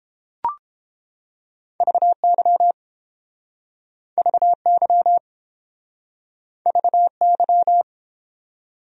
Morse code, heard as a keyed tone, about 32 wpm, 700 Hz. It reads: VY VY VY